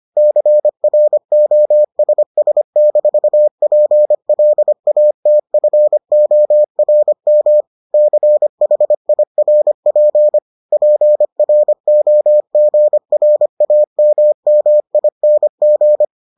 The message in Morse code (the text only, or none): CROSS-PLATFORM CHIRP PROGRAMMING